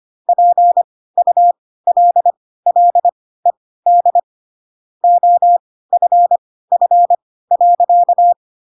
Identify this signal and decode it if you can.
Morse code, heard as a keyed tone, about 25 wpm, spaced as for 16 wpm, 700 Hz. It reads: PULLED OFF.
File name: Pinocchio / Pinocchio ch3 10 25 ik1hge